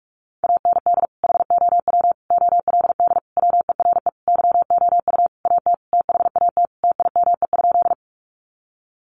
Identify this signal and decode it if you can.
Morse code, heard as a keyed tone, about 44 wpm, 700 Hz. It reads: AND HOW OLD WERE YOU AT THAT TIME?
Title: Sentences from Top 200 Words 45wpm